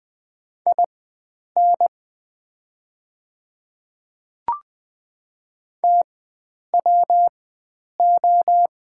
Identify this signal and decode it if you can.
Morse code, heard as a keyed tone, about 20 wpm, spaced as for 9 wpm, 690 Hz. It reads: IN TWO
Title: 4x Character Spacing - Top 100 Words - 20wpm